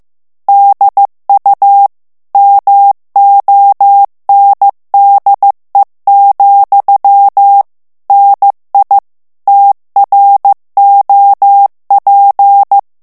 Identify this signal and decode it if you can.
Morse code, heard as a keyed tone, about 15 wpm, 800 Hz. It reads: DU MONDE, NI TROP
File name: FRANCAIS TEXTE CLAIR 2 VITESSE 14 MOTS MINUTES